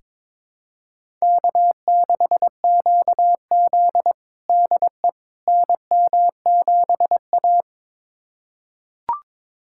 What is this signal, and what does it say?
Morse code, heard as a keyed tone, about 22 wpm, 700 Hz. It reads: K6QZ DE NM7A
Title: USA Callsign DE Callsign - 4 Characters - 22wpm